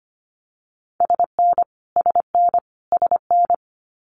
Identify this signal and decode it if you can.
Morse code, heard as a keyed tone, about 25 wpm, 700 Hz. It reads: SN SN SN
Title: Abbreviations Repeated 3x - 25wpm